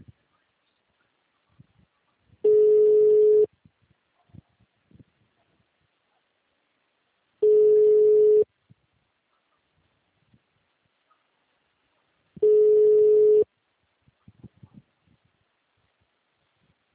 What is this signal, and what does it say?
Ringback tone (Europe, 425 Hz).
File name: out- -2011- - - .16